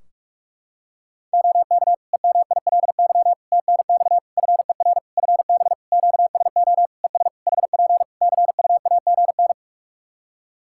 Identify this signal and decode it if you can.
Morse code, heard as a keyed tone, about 44 wpm, 700 Hz. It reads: OK EMILY TNX FER FB QSO ES HP CUAGN
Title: Practice QSO - Exchange 4 - Ending - 45wpm